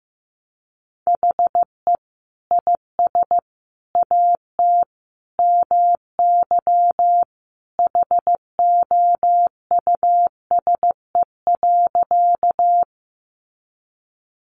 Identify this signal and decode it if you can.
Morse code, heard as a keyed tone, about 15 wpm, 700 Hz. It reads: HE IS AT MY HOUSE.
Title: Sentences from Top 200 Words 15wpm